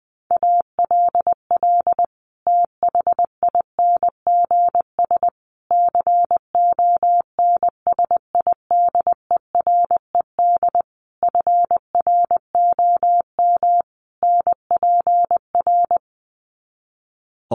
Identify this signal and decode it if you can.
Morse code, heard as a keyed tone, about 20 wpm, 700 Hz. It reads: ALL THINGS CONSIDERED FROM NPR
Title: Sets of 5 Words Encore 20wpm